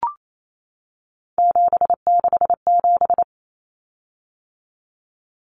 Morse code, heard as a keyed tone, about 28 wpm, 700 Hz. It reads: 767
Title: Common Cut Numbers - 3-Digit 28wpm